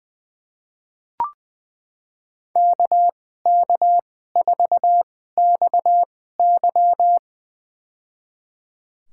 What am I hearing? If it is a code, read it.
Morse code, heard as a keyed tone, about 21 wpm, spaced as for 15 wpm, 700 Hz. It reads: KK4XY
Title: USA Call Signs - 5 Characters - 2x Character Spacing - 20wpm